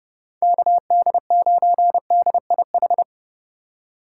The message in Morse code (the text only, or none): KD9DIH